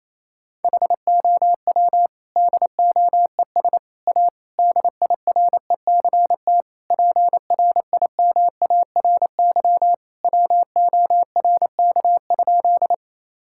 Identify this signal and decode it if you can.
Morse code, heard as a keyed tone, about 28 wpm, 700 Hz. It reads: HOW DOES A DIRECT PRIMARY WORK?
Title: Sentences from Top 1000 Words - Review Part 5 - 28wpm